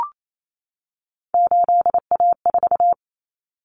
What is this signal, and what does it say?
Morse code, heard as a keyed tone, about 28 wpm, 700 Hz. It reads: 8A4
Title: Common Cut Numbers - 3-Digit 28wpm